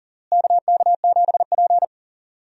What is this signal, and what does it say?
Morse code, heard as a keyed tone, about 37 wpm, 700 Hz. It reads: KK7P
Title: USA Call Signs - 4 Characters - Repeated 2x - 40wpm